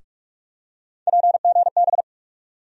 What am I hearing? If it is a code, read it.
Morse code, heard as a keyed tone, about 43 wpm, 700 Hz. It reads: PGB
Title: USA Call Sign Postfixes - 3 Letters 45wpm